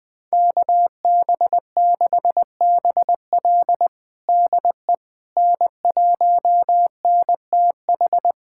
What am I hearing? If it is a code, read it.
Morse code, heard as a keyed tone, about 20 wpm, 700 Hz. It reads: KB6BL DE N1NTH
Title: USA Callsign DE Callsign - 5 Characters - 20wpm